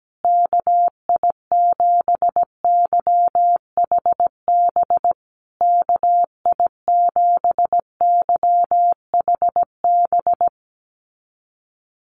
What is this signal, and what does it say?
Morse code, heard as a keyed tone, about 17 wpm, 700 Hz. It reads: KI7YHB KI7YHB
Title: USA Call Signs Repeated 3x 17wpm